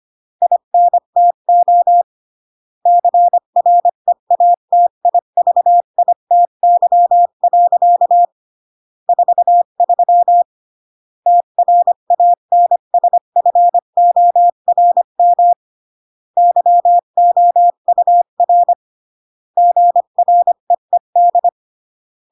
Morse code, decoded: INTO CREATIVITY. 43 TRANSFORM YOUR GREED